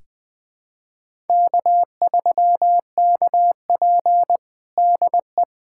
Morse code, heard as a keyed tone, about 20 wpm, 700 Hz. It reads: K3KP DE
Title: USA Callsign DE Callsign - 4 Characters - 20wpm